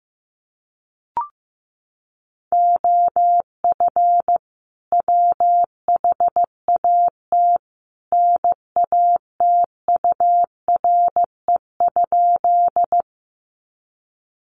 Morse code, heard as a keyed tone, about 15 wpm, 700 Hz. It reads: OF WHAT NATURE?
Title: Sentences from Top 600 Words 15wpm